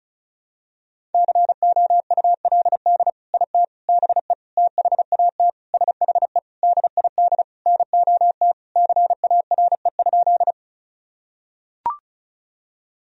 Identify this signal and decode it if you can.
Morse code, heard as a keyed tone, about 36 wpm, 700 Hz. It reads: COULD IT BE THAT SHE DID NOT CARE?
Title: Sentences from Top 1000 Words - Review Part 1 - 35wpm